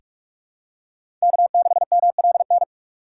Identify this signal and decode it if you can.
Morse code, heard as a keyed tone, about 47 wpm, 700 Hz. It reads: K6MLN
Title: USA Call Signs - 5 Characters - 3x Character Spacing - 45wpm